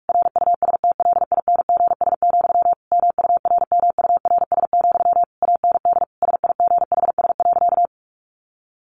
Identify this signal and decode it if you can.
Morse code, heard as a keyed tone, about 44 wpm, 700 Hz. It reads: RUSTLINGS, MURMURS, AND SIGHS.
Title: A Sound Of Thunder 45wpm